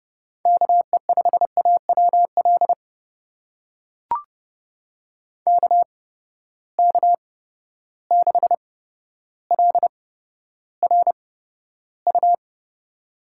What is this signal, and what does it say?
Morse code, heard as a keyed tone, about 30 wpm, spaced as for 13 wpm, 700 Hz. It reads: KE5AWL KK6LRU
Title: USA Call Signs - 8x Character Spacing - 30wpm